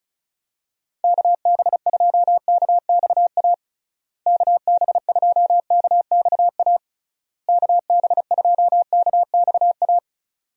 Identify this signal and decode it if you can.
Morse code, heard as a keyed tone, about 34 wpm, 700 Hz. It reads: KB2KXA KB2KXA KB2KXA
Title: USA Call Signs - 6 Characters - Repeated 3x - 35wpm